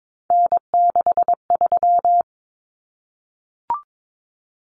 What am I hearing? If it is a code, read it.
Morse code, heard as a keyed tone, about 22 wpm, 700 Hz. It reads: N63
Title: Common Cut Numbers - 3-Digit 22wpm